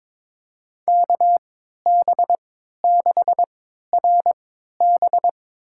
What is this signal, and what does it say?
Morse code, heard as a keyed tone, about 22 wpm, spaced as for 13 wpm, 700 Hz. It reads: KB6RB
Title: USA Call Signs - 5 Characters - 3x Character Spacing - 22wpm